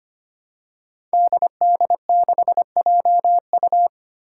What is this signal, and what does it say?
Morse code, heard as a keyed tone, about 25 wpm, 700 Hz. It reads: DD6JU
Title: Five Challenging Letter-Number Combinations 25wpm